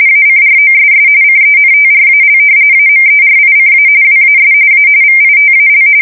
RTTY, about 45.45 baud. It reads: ACROSS CENTRAL AMERICA WAS A VITAL
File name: fsk170 / usnavy3